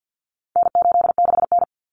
Morse code, heard as a keyed tone, about 50 wpm, 700 Hz. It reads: N86N
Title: Common Cut Numbers - 4-Digit 50wpm